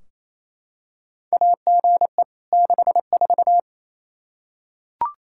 Morse code, heard as a keyed tone, about 27 wpm, 700 Hz. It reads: AGE 64
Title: Practice QSO - Exchange 3 - More Info - 28wpm